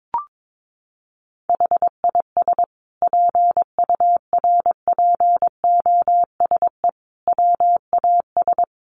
Morse code, heard as a keyed tone, about 22 wpm, 700 Hz. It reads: HIS PURPOSE WAS
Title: Sentences from Top 1100 Words 22wpm